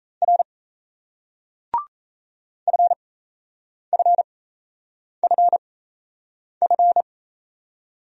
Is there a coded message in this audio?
Yes: Morse code from 0.2 to 7.0 s, heard as a keyed tone, about 38 wpm, spaced as for 8 wpm, 700 Hz. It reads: R FFFF